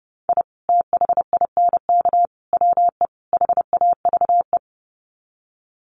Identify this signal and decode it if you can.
Morse code, heard as a keyed tone, about 30 wpm, 700 Hz. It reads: I THINK WE HAVE
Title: Sets of 4 Words 30wpm